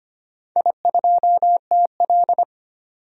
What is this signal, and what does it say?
Morse code, heard as a keyed tone, about 25 wpm, 700 Hz. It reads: I2TL
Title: DXCC-100 Call Signs - 4 Characters - 25wpm